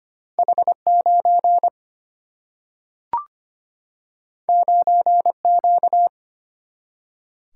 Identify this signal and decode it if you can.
Morse code, heard as a keyed tone, about 25 wpm, 700 Hz. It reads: H9 9Q
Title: Single-Letter Single-Number 25wpm